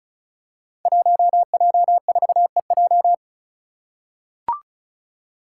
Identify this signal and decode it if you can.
Morse code, heard as a keyed tone, about 36 wpm, 700 Hz. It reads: 1J4EJ
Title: Five Challenging Letter-Number Combinations 35wpm